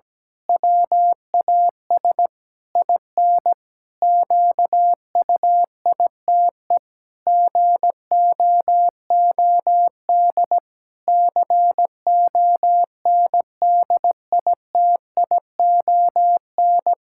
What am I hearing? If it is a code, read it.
Morse code, heard as a keyed tone, about 17 wpm, 700 Hz. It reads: WAS IN QUITE GOOD CONDITION
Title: Pangram Challenge v2 - 17wpm